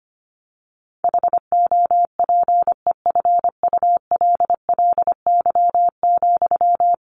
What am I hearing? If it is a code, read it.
Morse code, heard as a keyed tone, about 25 wpm, 700 Hz. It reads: HOPEFULLY,
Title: Sentences from Top 1500 Words 25wpm